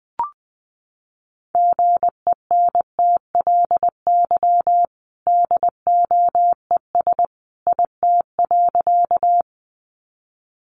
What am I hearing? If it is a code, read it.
Morse code, heard as a keyed tone, about 20 wpm, 700 Hz. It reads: GENTLY DOES IT.
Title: Sentences from Top 1500 Words 20wpm